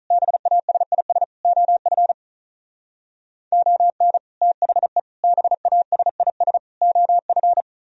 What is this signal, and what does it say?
Morse code, heard as a keyed tone, about 45 wpm, 700 Hz. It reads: BASIS OF ON THE BASIS OF